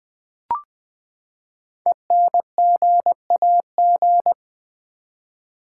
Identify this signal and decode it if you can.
Morse code, heard as a keyed tone, about 20 wpm, 700 Hz. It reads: ENGAG
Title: Build A Word - 5 Letters At A Time - 10-Letter Words - 20wpm